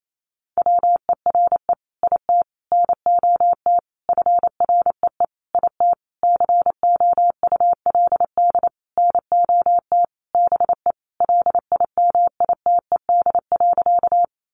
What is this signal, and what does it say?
Morse code, heard as a keyed tone, about 28 wpm, 700 Hz. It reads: WERE IT NOT FREE IT COULD NOT BE LIMITED.
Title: Sentences from Top 900 Words 28wpm